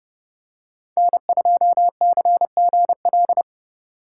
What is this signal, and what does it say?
Morse code, heard as a keyed tone, about 28 wpm, 700 Hz. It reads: N2CGL